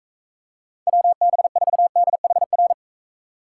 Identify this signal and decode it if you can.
Morse code, heard as a keyed tone, about 43 wpm, 700 Hz. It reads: WB4DHR